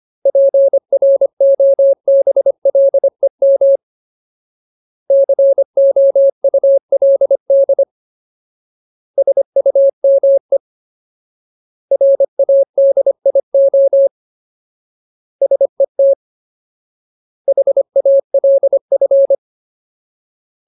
Morse code, heard as a keyed tone, about 25 wpm, 550 Hz. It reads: PROBLEM COULD SUME RADIO SET HALF
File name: Ultimate Word List / 25WPM Ultimate